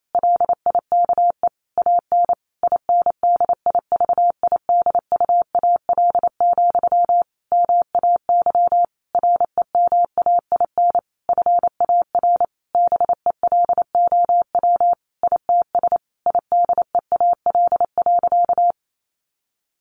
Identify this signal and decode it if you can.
Morse code, heard as a keyed tone, about 28 wpm, 700 Hz. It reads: LIKE AN INDIVIDUAL, MAY REMAIN FAR BELOW ITS IDEAL.
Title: Sentences from Top 1400 Words 28wpm